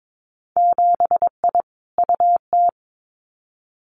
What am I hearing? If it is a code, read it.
Morse code, heard as a keyed tone, about 22 wpm, 700 Hz. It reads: 7I UT
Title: Winter Field Day Exchanges Encore 22wpm